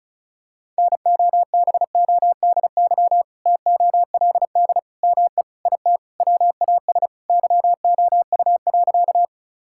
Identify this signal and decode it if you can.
Morse code, heard as a keyed tone, about 34 wpm, 700 Hz. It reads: NOBODY TOLD ME IT WAS YOU.
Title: Sentences from Top 1600 Words 35wpm